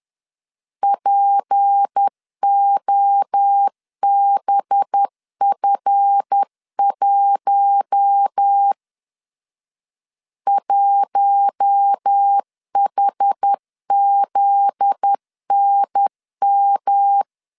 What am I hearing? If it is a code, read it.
Morse code, heard as a keyed tone, about 10 wpm, 800 Hz. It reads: POBF1 1HZNM
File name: LEZIONI / 7 GF9Z / 7 GF9Z 10wpm 2611